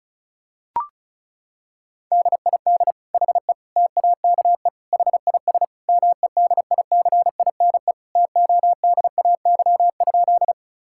Morse code, heard as a keyed tone, about 36 wpm, 700 Hz. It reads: DID HE TAKE HIS MEDICINE TODAY?
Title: Sentences from Top 1000 Words - Review Part 6 - 35wpm